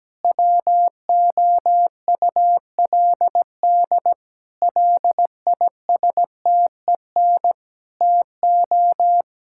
Morse code, heard as a keyed tone, about 17 wpm, 700 Hz. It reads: WOULD LISTEN TO
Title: Sentences from Top 1600 Words 17wpm